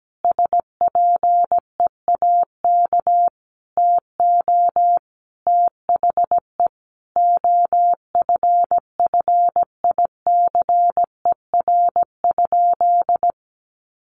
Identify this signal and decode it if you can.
Morse code, heard as a keyed tone, about 17 wpm, 700 Hz. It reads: SPEAK TO THE OFFICER?
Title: Sentences from Top 1200 Words 17wpm